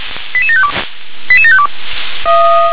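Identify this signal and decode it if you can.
Five-tone selective call (ZVEI), heard as quick runs of five tones, tones of about 70 ms. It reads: 88752 88752
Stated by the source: ZVEI five-tone selective call 88692, received by SDR.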